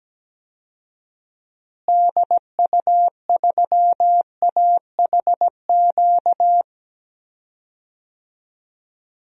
Morse code, heard as a keyed tone, about 17 wpm, 700 Hz. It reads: DU3AHQ
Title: DXCC-100 Call Signs - 6 Characters - Repeated 2x - 17wpm